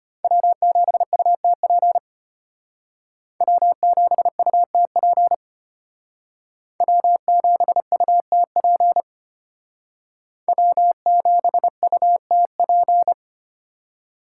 Morse code, heard as a keyed tone, about 33 wpm, 700 Hz. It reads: W7UTP W7UTP W7UTP W7UTP